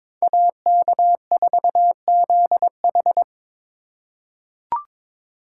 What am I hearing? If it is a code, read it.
Morse code, heard as a keyed tone, about 22 wpm, 700 Hz. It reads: AK4ZH
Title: USA Call Signs - 5 Characters - 3x Character Spacing - 22wpm